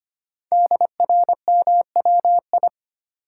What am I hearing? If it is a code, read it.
Morse code, heard as a keyed tone, about 25 wpm, 700 Hz. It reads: DRMWI
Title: Code Groups - 5 Letters - 25wpm